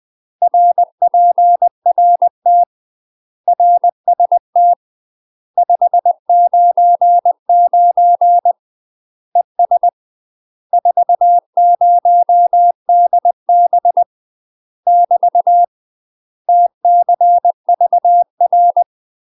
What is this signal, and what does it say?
Morse code, heard as a keyed tone, about 20 wpm, 700 Hz. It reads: RPRT RST 599 ES 40DB <BT> TCVR
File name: UK MASTER QSOs TRAINING/20WPM